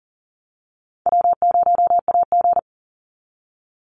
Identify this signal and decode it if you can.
Morse code, heard as a keyed tone, about 40 wpm, 700 Hz. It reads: W0AG
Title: Easy Call Signs - 40wpm